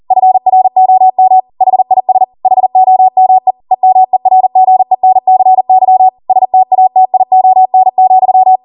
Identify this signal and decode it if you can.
Morse code, heard as a keyed tone, about 40 wpm, 750 Hz. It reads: FROM HIS HOME EMERGENCY STATION,